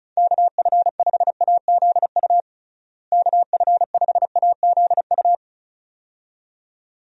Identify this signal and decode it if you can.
Morse code, heard as a keyed tone, about 36 wpm, 700 Hz. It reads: KF5AZU KF5AZU
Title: USA Call Signs - 6 Characters - Repeated 2x - 35wpm